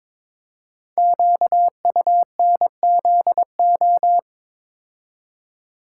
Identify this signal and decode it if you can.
Morse code, heard as a keyed tone, about 22 wpm, 700 Hz. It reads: QUNZO
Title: Code Groups - 5 Letters - 22wpm